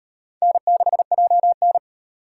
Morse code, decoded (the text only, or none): N6JN